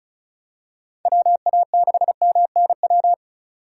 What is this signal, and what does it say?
Morse code, heard as a keyed tone, about 36 wpm, 700 Hz. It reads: WA6MNW
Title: USA Call Signs - 6 Characters - 2x Character Spacing - 35wpm